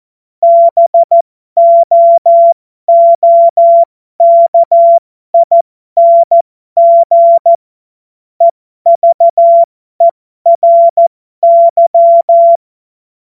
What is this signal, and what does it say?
Morse code, heard as a keyed tone, about 14 wpm, spaced as for 12 wpm, 680 Hz. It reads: BOOKING EVERY
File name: Morsekurs / Witze